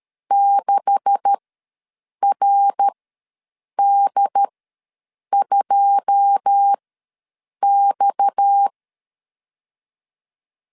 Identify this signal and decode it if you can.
Morse code, heard as a keyed tone, about 13 wpm, spaced as for 7 wpm, 800 Hz. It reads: 6RD2X
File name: LEZIONI / 10 SIGNS / 10A SLASH ws 12wpm 3443